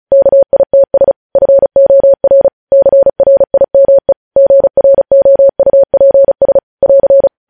Morse code, heard as a keyed tone, about 36 wpm, 550 Hz. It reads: KITS FOR CRIME GROUPS <AR>
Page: news headlines in morse code at 35 wpm